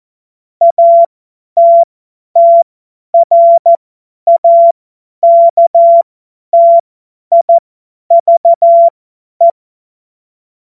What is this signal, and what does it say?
Morse code, heard as a keyed tone, about 14 wpm, spaced as for 10 wpm, 680 Hz. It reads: ATTRAKTIVE
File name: Morsekurs / Witze / Witz 12 März 2024 50 F70